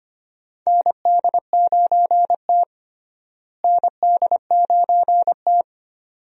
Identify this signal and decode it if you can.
Morse code, heard as a keyed tone, about 25 wpm, 700 Hz. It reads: ND9T ND9T
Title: USA Call Signs - 4 Characters - Repeated 2x - 25wpm